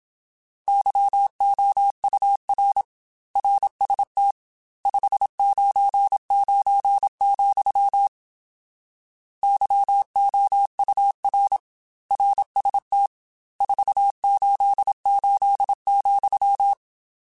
Morse code, heard as a keyed tone, about 27 wpm, 800 Hz. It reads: YOUR RST 599, YOUR RST 488,
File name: RSTs / RST-mp3s / Your RSTs @25